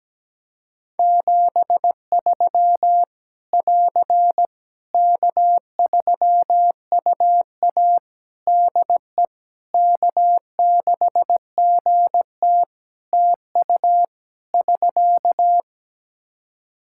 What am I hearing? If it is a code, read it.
Morse code, heard as a keyed tone, about 17 wpm, 700 Hz. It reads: 73 <AR> K3UA DE K6GT TU <SK>
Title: Practice QSO - Exchange 4 - Ending - 17wpm